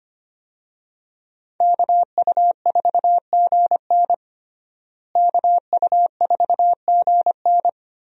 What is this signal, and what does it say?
Morse code, heard as a keyed tone, about 25 wpm, 700 Hz. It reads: KU4GN KU4GN
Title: USA Call Signs - 5 Characters - Repeated 3x - 25wpm